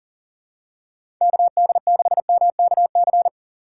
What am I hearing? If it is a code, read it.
Morse code, heard as a keyed tone, about 40 wpm, 700 Hz. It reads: KD6MKC
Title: USA Call Signs - 6 Characters - 2x Character Spacing - 40wpm